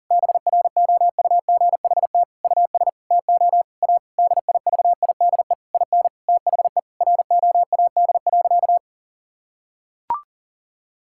Morse code, heard as a keyed tone, about 40 wpm, 700 Hz. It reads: BROUGHT US TO A DIVIDE IN THE ROAD.
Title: Sentences from Top 1600 Words 40wpm